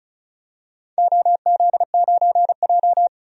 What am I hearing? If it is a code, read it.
Morse code, heard as a keyed tone, about 36 wpm, 700 Hz. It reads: OZ9J
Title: DXCC-100 Call Signs - 4 Characters - 35wpm